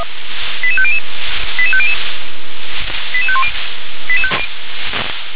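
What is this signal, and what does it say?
Five-tone selective call (ZVEI), heard as quick runs of five tones, tones of about 70 ms. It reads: 88500 88500 88510
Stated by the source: ZVEI five-tone selective call received by SDR.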